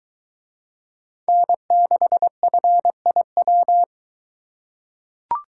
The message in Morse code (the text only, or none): N6FIW